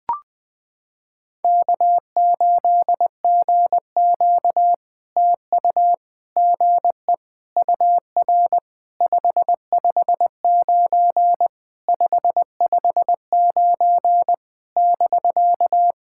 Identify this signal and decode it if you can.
Morse code, heard as a keyed tone, about 20 wpm, 700 Hz. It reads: K8GQ TU GE UR 559 559 <BK>
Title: POTA QSO Practice - 20wpm